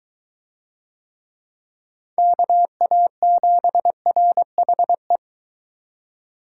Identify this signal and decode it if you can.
Morse code, heard as a keyed tone, about 23 wpm, 700 Hz. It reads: KA7RHE